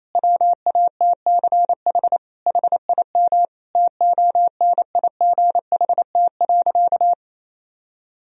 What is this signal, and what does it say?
Morse code, heard as a keyed tone, about 28 wpm, 690 Hz. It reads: WATCH HIM TONIGHT.
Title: Sentences from Top 1000 Words - Review Part 3 - 28wpm